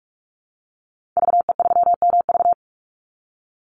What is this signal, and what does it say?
Morse code, heard as a keyed tone, about 44 wpm, 700 Hz. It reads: VE3MV
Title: Call Signs - 45wpm